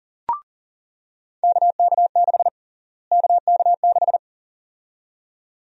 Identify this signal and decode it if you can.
Morse code, heard as a keyed tone, about 40 wpm, 700 Hz. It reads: KK6 KK6
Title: USA Call Sign Prefixes - Repeated 2x - 40wpm